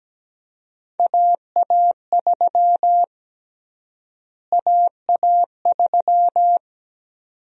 Morse code, heard as a keyed tone, about 17 wpm, 700 Hz. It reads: AA3 AA3
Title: USA Call Sign Prefixes - Repeated 2x - 17wpm